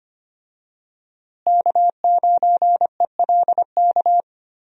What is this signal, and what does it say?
Morse code, heard as a keyed tone, about 25 wpm, 700 Hz. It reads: K9ELK
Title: USA Call Signs - 5 Characters - Repeated 2x - 25wpm